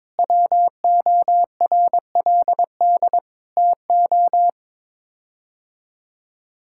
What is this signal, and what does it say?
Morse code, heard as a keyed tone, about 23 wpm, 700 Hz. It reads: WORLD TO
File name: practice / 22wpm / Sets of 8 Words - 22wpm